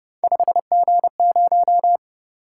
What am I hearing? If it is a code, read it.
Morse code, heard as a keyed tone, about 30 wpm, 700 Hz. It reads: 5G0